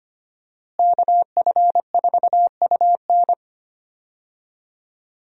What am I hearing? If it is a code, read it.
Morse code, heard as a keyed tone, about 25 wpm, 700 Hz. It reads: KF4UN
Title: USA Call Signs - 5 Characters - Repeated 3x - 25wpm